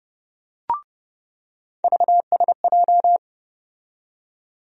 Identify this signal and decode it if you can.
Morse code, heard as a keyed tone, about 30 wpm, 700 Hz. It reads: VSJ
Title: USA Call Sign Postfixes - 3 Letters 30wpm